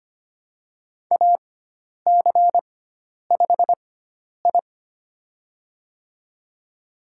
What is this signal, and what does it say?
Morse code, heard as a keyed tone, about 25 wpm, spaced as for 10 wpm, 700 Hz. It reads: AC5I
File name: Lesson 16g - Letter c - Call Signs 25wpm